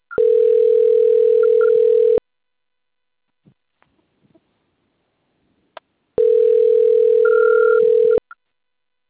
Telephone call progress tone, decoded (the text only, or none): ringback tone